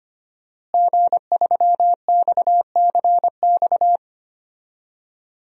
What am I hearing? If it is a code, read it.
Morse code, heard as a keyed tone, about 25 wpm, 700 Hz. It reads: G3XCX
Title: DXCC-100 Call Signs - 5 Characters - Repeated 3x - 25wpm